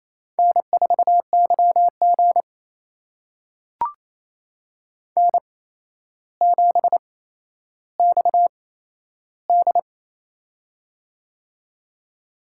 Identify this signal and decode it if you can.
Morse code, heard as a keyed tone, about 28 wpm, spaced as for 12 wpm, 700 Hz. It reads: N4YG N7XD
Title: USA Call Signs - 4 Characters - 8x Character Spacing - 28wpm